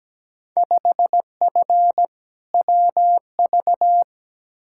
Morse code, heard as a keyed tone, about 17 wpm, 700 Hz. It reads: 5F WV
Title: Field Day Exchanges Encore - 17wpm